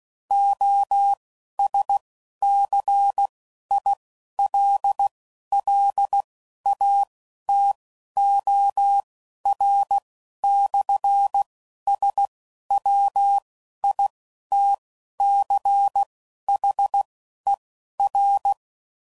Morse code, heard as a keyed tone, about 16 wpm, spaced as for 11 wpm, 800 Hz. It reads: OSCILLATOR/SWITCHER
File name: CW Code Groups @10WPM / Technical Subjects / SwPwrSup